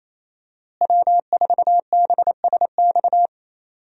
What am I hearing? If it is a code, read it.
Morse code, heard as a keyed tone, about 28 wpm, 700 Hz. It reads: W4BSX
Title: USA Call Signs - 5 Characters - Repeated 2x - 28wpm